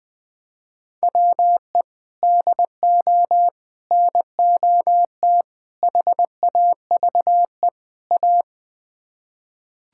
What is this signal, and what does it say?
Morse code, heard as a keyed tone, about 20 wpm, 700 Hz. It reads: WE DO NOT HAVE A
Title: Build A Phrase - 5-Word Phrases - 20wpm